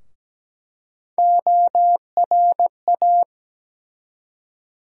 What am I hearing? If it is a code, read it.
Morse code, heard as a keyed tone, about 17 wpm, 700 Hz. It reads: ORA
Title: USA Call Sign Postfixes - 3 Letters 17wpm